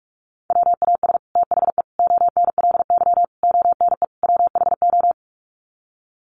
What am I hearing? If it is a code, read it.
Morse code, heard as a keyed tone, about 44 wpm, 700 Hz. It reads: WAS THE ONLY ONE WHO
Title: Sets of 5 Words Encore2 45wpm